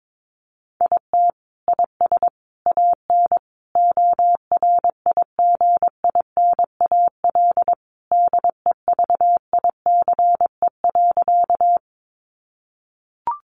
Morse code, heard as a keyed tone, about 22 wpm, 700 Hz. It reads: IT IS AN ORIGINAL DEVICE.
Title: Sentences from Top 1500 Words 22wpm